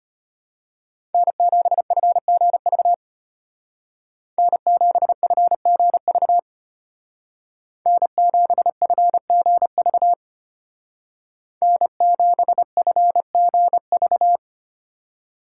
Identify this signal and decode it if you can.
Morse code, heard as a keyed tone, about 33 wpm, 700 Hz. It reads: N7FGV N7FGV N7FGV N7FGV